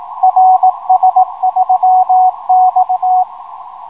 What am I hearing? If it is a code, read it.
Morse code, heard as a keyed tone, about 18 wpm, 780 Hz, with radio noise. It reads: RS3X
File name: rs-12 robot